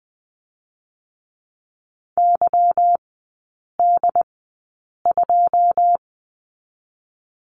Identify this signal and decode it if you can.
Morse code, heard as a keyed tone, about 20 wpm, 700 Hz. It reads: Y D 2
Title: 198 Lesson 32e - Letter k - Three Characters 20wpm